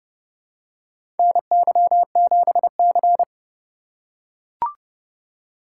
Morse code, heard as a keyed tone, about 30 wpm, 700 Hz. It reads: NY7C
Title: USA Call Signs - 4 Characters - 4x Character Spacing - 30wpm